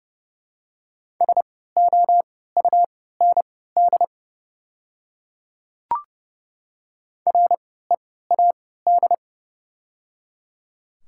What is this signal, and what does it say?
Morse code, heard as a keyed tone, about 30 wpm, spaced as for 17 wpm, 700 Hz. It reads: SOUND READ
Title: 3x Character Spacing - Top 200 Words - 30wpm